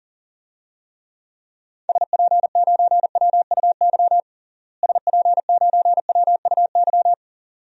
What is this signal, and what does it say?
Morse code, heard as a keyed tone, about 40 wpm, 700 Hz. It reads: SP9WUY SP9WUY
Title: DXCC-100 Call Signs - 6 Characters - Repeated 2x - 40wpm